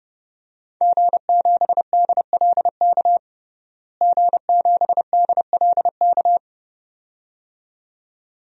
Morse code, heard as a keyed tone, about 30 wpm, 700 Hz. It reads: G7DLK G7DLK
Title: DXCC-100 Call Signs - 5 Characters - Repeated 2x - 30wpm